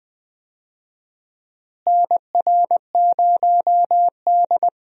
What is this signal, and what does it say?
Morse code, heard as a keyed tone, about 21 wpm, 700 Hz. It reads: NR0D
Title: USA Call Signs - 4 Characters - Repeated 3x - 20wpm